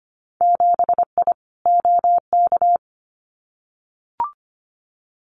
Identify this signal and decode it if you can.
Morse code, heard as a keyed tone, about 25 wpm, 700 Hz. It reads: 7I OK
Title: Winter Field Day Exchanges Encore 25wpm